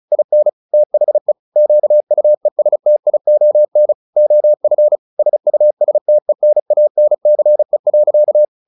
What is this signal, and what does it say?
Morse code, heard as a keyed tone, about 34 wpm, 600 Hz. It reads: IN THE QUESTION OF SUSTENANCE.
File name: the old man and the sea 2 35wpm